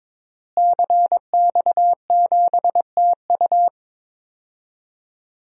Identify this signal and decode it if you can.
Morse code, heard as a keyed tone, about 22 wpm, 700 Hz. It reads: CX7TU
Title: DXCC-100 Call Signs - 5 Characters - Repeated 3x - 22wpm